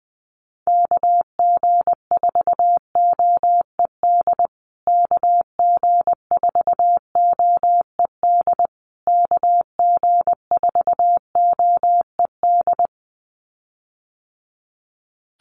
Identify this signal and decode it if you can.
Morse code, heard as a keyed tone, about 20 wpm, 700 Hz. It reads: KG4OED KG4OED KG4OED
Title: USA Call Signs Repeated 3x 20wpm